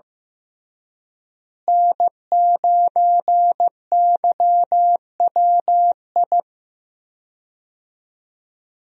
Morse code, heard as a keyed tone, about 15 wpm, 700 Hz. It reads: N9YWI
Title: USA Call Signs - 5 Characters - Repeated 3x - 15wpm